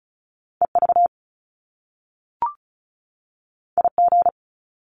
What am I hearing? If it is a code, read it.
Morse code, heard as a keyed tone, about 36 wpm, 700 Hz. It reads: EV IG
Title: Top Two Letter Combinations in Words 35wpm